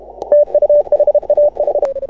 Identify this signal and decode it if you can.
Morse code, heard as a keyed tone, about 30 wpm, 590 Hz. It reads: TU HAH
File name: AUDIO2018 / HA6OI CW